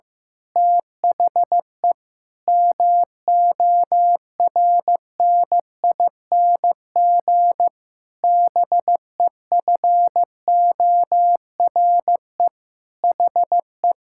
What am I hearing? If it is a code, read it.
Morse code, heard as a keyed tone, about 15 wpm, 700 Hz. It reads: THE MORNING BEFORE HE
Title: Pangram Challenge v2 - 15wpm